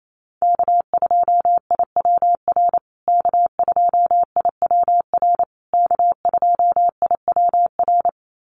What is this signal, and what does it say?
Morse code, heard as a keyed tone, about 28 wpm, 700 Hz. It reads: K2IWR K2IWR K2IWR
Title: USA Call Signs Repeated 3x 28wpm